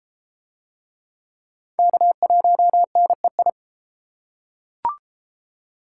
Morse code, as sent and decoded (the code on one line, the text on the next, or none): -.- .---- -. ..
K1NEI